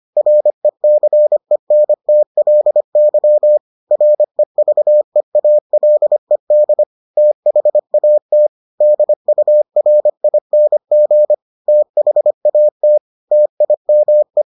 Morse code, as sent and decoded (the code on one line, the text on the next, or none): .-. -.-. -. - .-.. -.-- / .-. ...- . .- .-.. -.. / - .... .- - / -.. ..- .-. .. -. --. / - .... .- - / - .. -- .
RECENTLY REVEALED THAT DURING THAT TIME